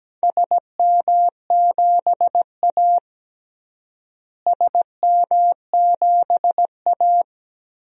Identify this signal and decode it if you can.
Morse code, heard as a keyed tone, about 17 wpm, 700 Hz. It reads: SM7A SM7A
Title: DXCC-100 Call Signs - 4 Characters - Repeated 3x - 17wpm